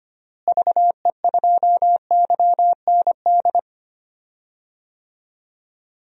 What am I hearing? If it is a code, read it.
Morse code, heard as a keyed tone, about 25 wpm, 700 Hz. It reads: VE2YND
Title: DXCC-100 Call Signs - 6 Characters - Repeated 3x - 25wpm